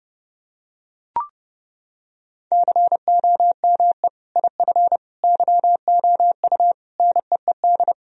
Morse code, heard as a keyed tone, about 30 wpm, 700 Hz. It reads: COME IF YOU NEED
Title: Sentences from Top 1000 Words - Review Part 1 - 30wpm